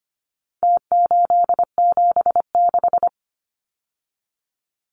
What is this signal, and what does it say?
Morse code, heard as a keyed tone, about 25 wpm, 700 Hz. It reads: T876
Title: Numbers - Contest Style - 4-Digit 25wpm